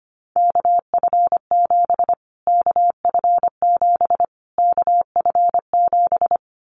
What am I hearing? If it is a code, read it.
Morse code, heard as a keyed tone, about 25 wpm, 700 Hz. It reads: KF7 KF7 KF7
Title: Callsign Prefixes for Top 25 DXCC - Repeated 3x - 25wpm